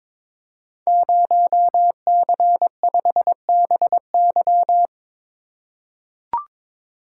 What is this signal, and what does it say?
Morse code, heard as a keyed tone, about 22 wpm, 700 Hz. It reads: 0C5BY
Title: Code Groups - 5 Letter:Numbers - 22wpm